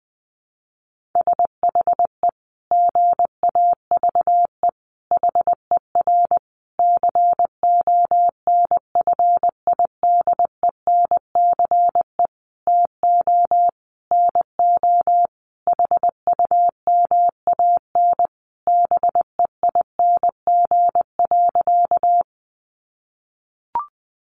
Morse code, heard as a keyed tone, about 20 wpm, 700 Hz. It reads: SHE GAVE HER CONFIDENCE TO NO HUMAN BEING.
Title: Sentences from Top 1500 Words 20wpm